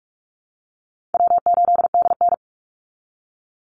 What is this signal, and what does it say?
Morse code, heard as a keyed tone, about 44 wpm, 700 Hz. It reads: W8DN
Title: Easy Call Signs Repeated 3x - 45wpm